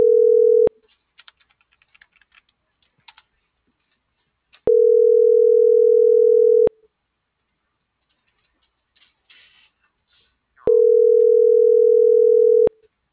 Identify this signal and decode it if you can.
Ringback tone (North America), 440 + 480 Hz.